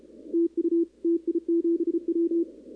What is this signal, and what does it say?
Morse code, heard as a keyed tone, about 30 wpm, 340 Hz, with radio noise. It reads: TU TI7W
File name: contests / 2019-cw-wpx-kt5j-k5pi- -2330 0-11